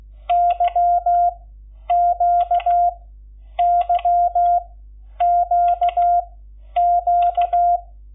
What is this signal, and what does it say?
Morse code, heard as a keyed tone, about 16 wpm, spaced as for 9 wpm, 700 Hz. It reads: YQYQQ
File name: Lecciones / CLASE Nº11